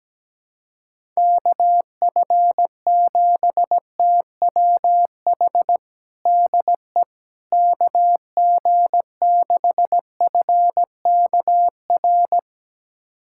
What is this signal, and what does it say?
Morse code, heard as a keyed tone, about 17 wpm, 700 Hz. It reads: KF7TWH DE KG6FKR
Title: USA Callsign DE Callsign - 6 Characters - 17wpm